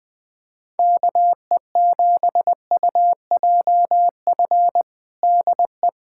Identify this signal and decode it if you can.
Morse code, heard as a keyed tone, about 20 wpm, 700 Hz. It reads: KE7UJF DE
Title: USA Callsign DE Callsign - 6 Characters - 20wpm